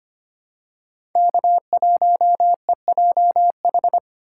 Morse code, heard as a keyed tone, about 25 wpm, 700 Hz. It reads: K1EJH
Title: USA Call Signs - 2x Character Spacing - 25wpm